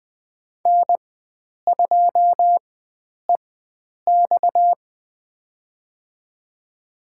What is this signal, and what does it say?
Morse code, heard as a keyed tone, about 20 wpm, spaced as for 9 wpm, 700 Hz. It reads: N2EX
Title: USA Call Signs - 4x Character Spacing - 20wpm